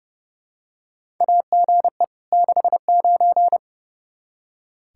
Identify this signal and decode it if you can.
Morse code, heard as a keyed tone, about 30 wpm, 700 Hz. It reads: AGE 69
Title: Practice QSO - Exchange 3 - More Info - 30wpm